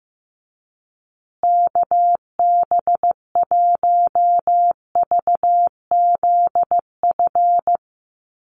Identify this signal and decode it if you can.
Morse code, heard as a keyed tone, about 15 wpm, 700 Hz. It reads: KB1VZF
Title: USA Call Signs Repeated 2x 15wpm